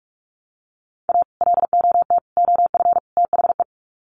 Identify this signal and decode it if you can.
Morse code, heard as a keyed tone, about 44 wpm, 700 Hz. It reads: A LOT OF THE
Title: Sets of 4 Words 45wpm